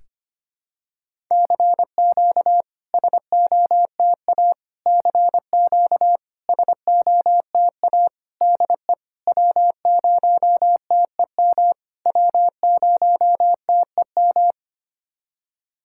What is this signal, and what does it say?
Morse code, heard as a keyed tone, about 25 wpm, 690 Hz. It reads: CQ SOTA CQ SOTA DE W0TEM W0TEM
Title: SOTA QSO Practice 25wpm